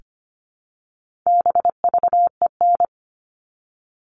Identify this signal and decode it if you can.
Morse code, heard as a keyed tone, about 25 wpm, 700 Hz. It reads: BVEN
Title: All Cut Numbers - 4-Digit 25wpm